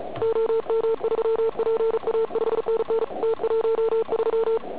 Morse code, heard as a keyed tone, about 36 wpm, 450 Hz, with noise. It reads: OM3PA 5NN T13
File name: Archiv / CUC106 / cuc106 PILEUP